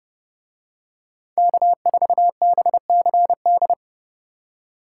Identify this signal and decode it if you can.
Morse code, heard as a keyed tone, about 29 wpm, 700 Hz. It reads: K4BCD